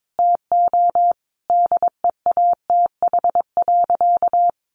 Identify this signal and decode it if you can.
Morse code, heard as a keyed tone, about 22 wpm, 700 Hz. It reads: TO DEATH.
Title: Sentences from Top 1400 Words 22wpm